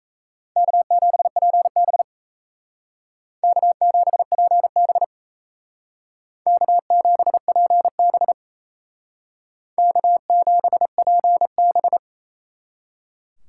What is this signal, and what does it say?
Morse code, heard as a keyed tone, about 37 wpm, 700 Hz. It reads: K7PB K7PB K7PB K7PB